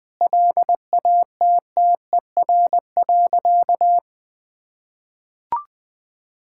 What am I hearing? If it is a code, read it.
Morse code, heard as a keyed tone, about 21 wpm, 700 Hz. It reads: LATTER.
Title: Sentences from Top 1600 Words 20wpm